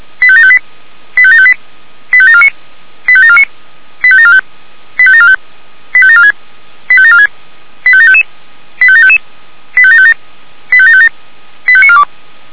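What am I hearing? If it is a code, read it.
Five-tone selective call (ZVEI), heard as quick runs of five tones, tones of about 70 ms. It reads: 85758 85758 85749 85749 85745 85745 85746 85746 85755 85755 85757 85757 85942
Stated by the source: ZVEI five-tone selective call received by SDR.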